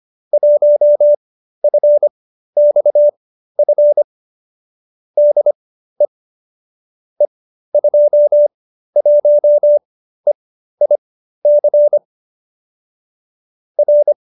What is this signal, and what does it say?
Morse code, heard as a keyed tone, about 25 wpm, spaced as for 13 wpm, 590 Hz. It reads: 1FXF DE E21EIC R
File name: qso208b 13